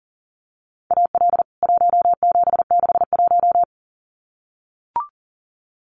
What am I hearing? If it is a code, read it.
Morse code, heard as a keyed tone, about 40 wpm, 700 Hz. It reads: AL 1761
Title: CWT Exchange 40wpm